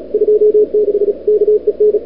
Morse code, heard as a keyed tone, about 35 wpm, 440 Hz. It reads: 26 KEN
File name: n6tr / Fab2007CWSprintHour1